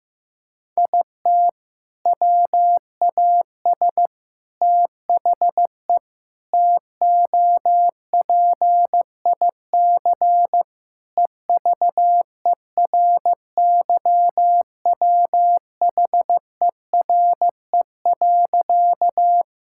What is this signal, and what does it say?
Morse code, heard as a keyed tone, about 15 wpm, 700 Hz. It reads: IT WAS THE TOPIC EVERYWHERE.
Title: Sentences from Top 1600 Words 15wpm